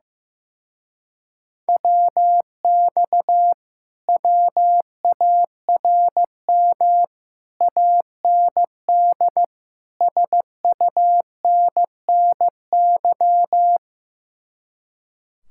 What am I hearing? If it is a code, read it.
Morse code, heard as a keyed tone, about 15 wpm, 700 Hz. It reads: WX WARM AND SUNNY
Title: Speed-Racing Common QSO Phrases Encore - 15wpm